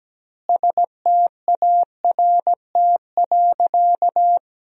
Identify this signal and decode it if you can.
Morse code, heard as a keyed tone, about 17 wpm, 700 Hz. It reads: START.
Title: 2x Word-Spacing - Sentences from Top 300 Words - 17wpm